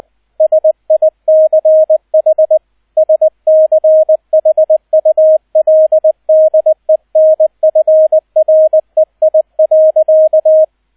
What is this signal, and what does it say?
Morse code, heard as a keyed tone, about 20 wpm, 630 Hz, with radio noise. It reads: SICH SCHULDENFREI.